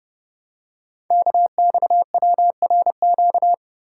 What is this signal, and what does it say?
Morse code, heard as a keyed tone, about 30 wpm, 700 Hz. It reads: KXWRQ
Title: Code Groups - 5 Letters - 30wpm